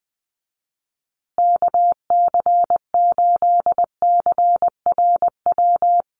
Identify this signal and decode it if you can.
Morse code, heard as a keyed tone, about 19 wpm, 700 Hz. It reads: KC8CRW
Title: USA Call Signs Repeated 2x 20wpm